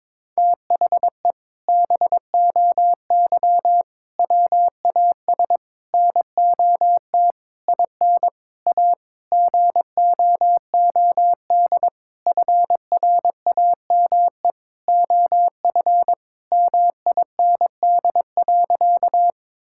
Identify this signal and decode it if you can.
Morse code, heard as a keyed tone, about 21 wpm, 700 Hz. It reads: THE BOY WAS NOT IN A GOOD FRAME OF MIND.
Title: Sentences from Top 1000 Words - Review Part 6 - 22wpm